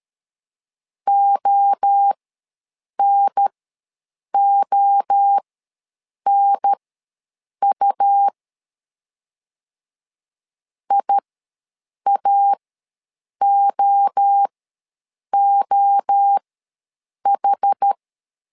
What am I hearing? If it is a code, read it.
Morse code, heard as a keyed tone, about 13 wpm, spaced as for 7 wpm, 800 Hz. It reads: ONONU IAOOH
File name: LEZIONI / 3 MWRO / 3 MWRO ws 12wpm 1914